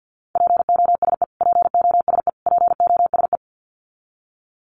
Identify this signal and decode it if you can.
Morse code, heard as a keyed tone, about 50 wpm, 700 Hz. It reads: POSE POSE POSE
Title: Top 500 Words Repeated 50wpm